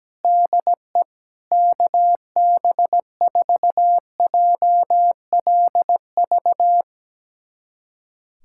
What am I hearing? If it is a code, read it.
Morse code, heard as a keyed tone, about 17 wpm, 700 Hz. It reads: DE KB4JLV
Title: USA Callsign DE Callsign - 6 Characters - 17wpm